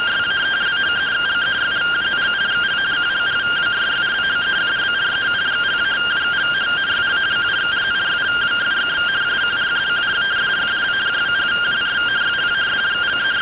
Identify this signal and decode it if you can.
Two-tone teleprinter signal, about 170 Hz shift.
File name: findings / axial iodine 10mm 184.gen-33 rtty rx